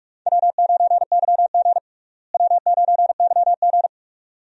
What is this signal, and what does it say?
Morse code, heard as a keyed tone, about 43 wpm, 700 Hz. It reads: W9YG W9YG
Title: USA Call Signs - 4 Characters - Repeated 2x - 45wpm